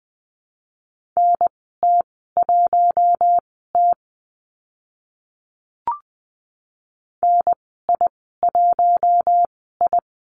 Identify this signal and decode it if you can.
Morse code, heard as a keyed tone, about 20 wpm, spaced as for 14 wpm, 700 Hz. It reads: NT1T NI1I
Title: 042 Lesson 8g - Number 1 - Call Signs 20wpm